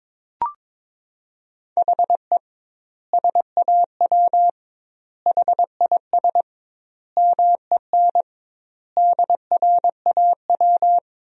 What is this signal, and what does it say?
Morse code, heard as a keyed tone, about 22 wpm, 700 Hz. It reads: HE SAW HIS MEN DRAW
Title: 2x Word-Spacing - Sentences from Top 300 Words - 22wpm